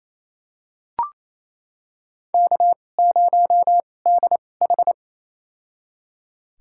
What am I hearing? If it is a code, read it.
Morse code, heard as a keyed tone, about 28 wpm, spaced as for 20 wpm, 700 Hz. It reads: K0DH
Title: USA Call Signs - 2x Character Spacing - 28wpm